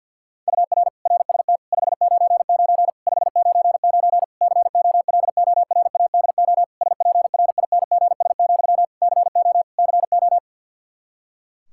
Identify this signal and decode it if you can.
Morse code, heard as a keyed tone, about 52 wpm, 700 Hz. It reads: UR RST 599 599 COLORADO SPRINGS, CO CO
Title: K1USN SST QSO Practice 50wpm